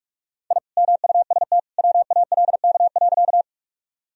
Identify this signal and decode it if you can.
Morse code, heard as a keyed tone, about 42 wpm, 700 Hz. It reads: I MUST WALK.
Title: Sentences from Top 1000 Words - Review Part 1 - 45wpm